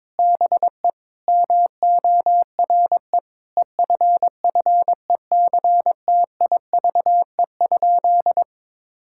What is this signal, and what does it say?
Morse code, heard as a keyed tone, about 22 wpm, 700 Hz. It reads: BE MORE EFFECTIVE?
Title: Sentences from Top 1000 Words - Review Part 6 - 22wpm